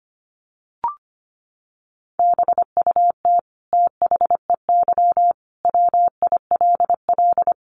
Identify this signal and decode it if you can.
Morse code, heard as a keyed tone, about 25 wpm, 700 Hz. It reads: BUT THEY WILL
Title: Sentences from Top 800 Words 25wpm